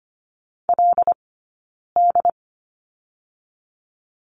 Morse code, heard as a keyed tone, about 25 wpm, 700 Hz. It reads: L D
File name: course / 25:10wpm / 092 Lesson 16d - Letter c - Two Characters 25wpm